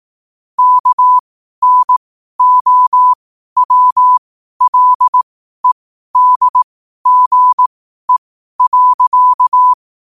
Morse code, heard as a keyed tone, about 18 wpm, spaced as for 12 wpm, 1000 Hz. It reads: KNOWLEDGE.
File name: qotdfiles / 12wpm